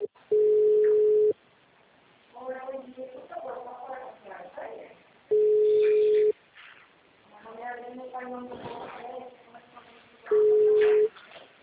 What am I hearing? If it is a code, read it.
Ringback tone (Europe, 425 Hz), with speech.